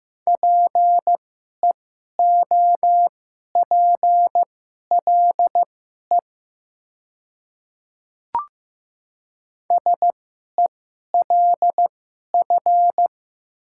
Morse code, heard as a keyed tone, about 15 wpm, spaced as for 11 wpm, 700 Hz. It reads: PEOPLE SELF
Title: 2x Character Spacing - Top 200 Words - 15wpm